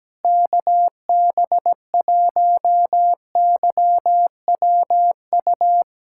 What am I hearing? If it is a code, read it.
Morse code, heard as a keyed tone, about 17 wpm, 700 Hz. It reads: KB1YWU
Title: USA Callsign DE Callsign - 6 Characters - 17wpm